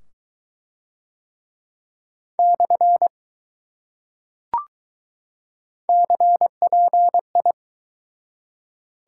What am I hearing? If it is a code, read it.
Morse code, heard as a keyed tone, about 21 wpm, 700 Hz. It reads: / CPI